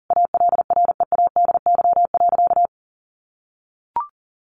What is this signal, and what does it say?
Morse code, heard as a keyed tone, about 40 wpm, 700 Hz. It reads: ALREADY.
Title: Sentences from Top 700 Words 40wpm